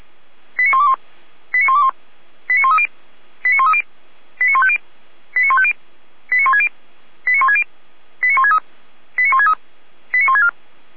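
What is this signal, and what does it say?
Five-tone selective call (ZVEI), heard as quick runs of five tones, tones of about 70 ms. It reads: 89121 89121 89130 89130 89150 89150 89160 89160 89163 89163 89164
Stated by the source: ZVEI five-tone selective call 89140 received by SDR, from the feuerwehr-orlamuende.org fire brigade server.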